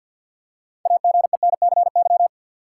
Morse code, heard as a keyed tone, about 53 wpm, 700 Hz. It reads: AGENCY
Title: Six Letter Words - 50wpm